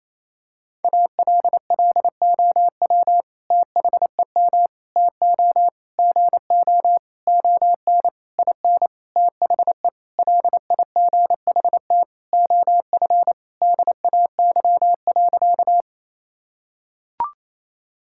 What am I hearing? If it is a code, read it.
Morse code, heard as a keyed tone, about 27 wpm, 700 Hz. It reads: ALLOW THEM TO GO ON IN THE LIGHT OF DAY.
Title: Sentences from Top 1000 Words - Review Part 5 - 28wpm